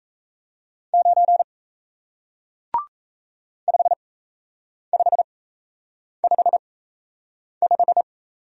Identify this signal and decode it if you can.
Morse code, heard as a keyed tone, about 38 wpm, spaced as for 8 wpm, 700 Hz. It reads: 9 5555